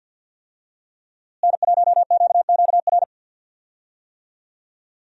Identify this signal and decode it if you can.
Morse code, heard as a keyed tone, about 47 wpm, 700 Hz. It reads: N1QQR
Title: USA Call Signs - 5 Characters - Repeated 2x - 50wpm